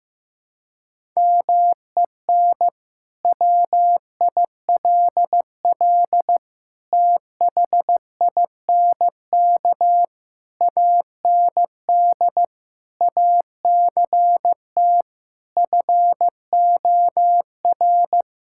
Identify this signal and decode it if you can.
Morse code, heard as a keyed tone, about 15 wpm, 700 Hz. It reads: MEN WILL THINK AND ACT FOR
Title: Sentences from Top 1000 Words - Review Part 5 - 15wpm